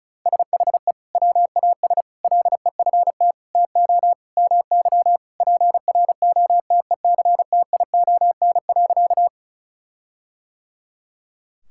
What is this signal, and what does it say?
Morse code, heard as a keyed tone, about 34 wpm, 700 Hz. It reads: SHE WAS LEFT TO MY PROTECTION.
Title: Sentences from Top 1600 Words 35wpm